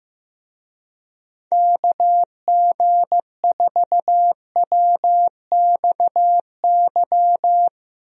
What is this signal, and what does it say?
Morse code, heard as a keyed tone, about 15 wpm, 700 Hz. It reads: KG4WXY
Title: USA Call Signs - 6 Characters - Repeated 3x - 15wpm